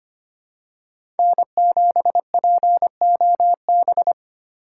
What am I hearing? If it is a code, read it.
Morse code, heard as a keyed tone, about 25 wpm, 700 Hz. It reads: N7POB
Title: USA Call Signs - 5 Characters - 3x Character Spacing - 25wpm